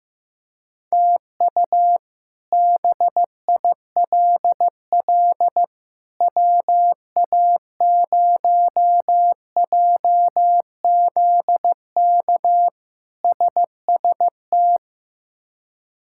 Morse code, heard as a keyed tone, about 16 wpm, 700 Hz. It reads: TU BILL WA0JZK SST
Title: K1USN SST QSO Practice - 15wpm